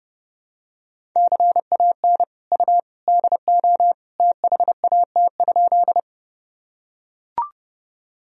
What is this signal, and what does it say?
Morse code, heard as a keyed tone, about 31 wpm, 700 Hz. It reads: CAN U DO THAT?
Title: Sentences from Top 100 Words and Abbreviations 30wpm